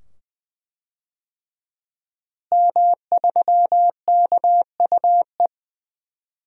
Morse code, heard as a keyed tone, about 21 wpm, 700 Hz. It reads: M3KUE
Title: DXCC-100 Call Signs - 5 Characters - Repeated 2x - 20wpm